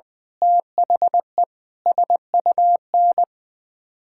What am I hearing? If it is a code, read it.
Morse code, heard as a keyed tone, about 20 wpm, 700 Hz. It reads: THE SUN
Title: Sentences with 7-Word Phrases - 20wpm